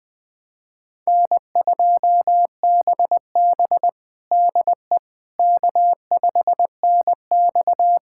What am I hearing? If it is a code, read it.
Morse code, heard as a keyed tone, about 20 wpm, 700 Hz. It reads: N2BB DE K5NX
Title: USA Callsign DE Callsign - 4 Characters - 20wpm